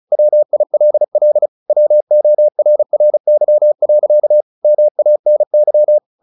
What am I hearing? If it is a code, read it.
Morse code, heard as a keyed tone, about 34 wpm, 600 Hz. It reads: WILL WORRY. MANY
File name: the old man and the sea 2 35wpm